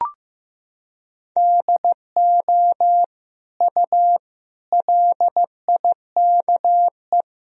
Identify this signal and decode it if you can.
Morse code, heard as a keyed tone, about 16 wpm, 700 Hz. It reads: DO U LIKE
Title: Sentences from Top 100 Words and Abbreviations 15wpm